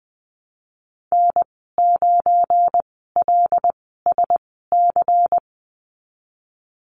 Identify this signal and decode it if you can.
Morse code, heard as a keyed tone, about 20 wpm, spaced as for 15 wpm, 700 Hz. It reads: N9LSC
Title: 179 Lesson 29g - Slash - Call Signs 20wpm